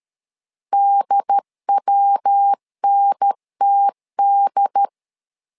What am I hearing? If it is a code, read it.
Morse code, heard as a keyed tone, about 13 wpm, 800 Hz. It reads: DWNTD
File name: LEZIONI / 3 MWRO / 3 MW 12wpm 1633